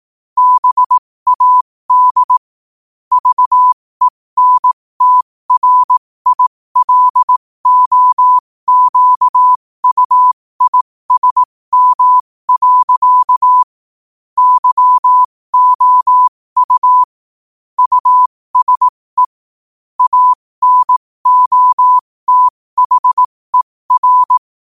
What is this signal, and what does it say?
Morse code, heard as a keyed tone, about 18 wpm, spaced as for 15 wpm, 1000 Hz. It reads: BAD VENTRILOQUISM. YOU USE ANOTHER
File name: qotdfiles / 15wpm